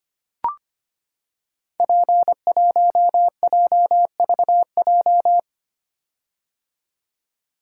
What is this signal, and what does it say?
Morse code, heard as a keyed tone, about 25 wpm, 700 Hz. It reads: P1JVJ
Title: Code Groups - 5 Letter:Numbers - 25wpm